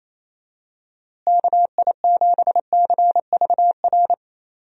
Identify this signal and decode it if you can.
Morse code, heard as a keyed tone, about 28 wpm, 700 Hz. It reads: KI7CVR
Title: USA Call Signs - 6 Characters - Repeated 2x - 28wpm